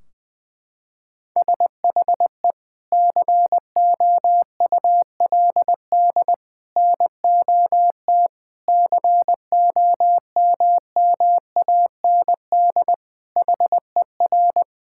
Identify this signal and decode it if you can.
Morse code, heard as a keyed tone, about 21 wpm, 700 Hz. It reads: SHE COULD NOT COMMAND HER
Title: Sentences from Top 1600 Words 20wpm